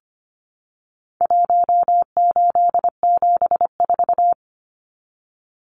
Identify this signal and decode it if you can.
Morse code, heard as a keyed tone, about 25 wpm, 700 Hz. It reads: 1874
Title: Numbers - Contest Style - 4-Digit 25wpm